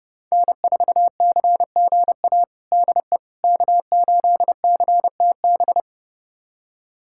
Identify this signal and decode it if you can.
Morse code, heard as a keyed tone, about 30 wpm, 700 Hz. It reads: N4CGA DE K8CTB
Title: USA Callsign DE Callsign - 5 Characters - 30wpm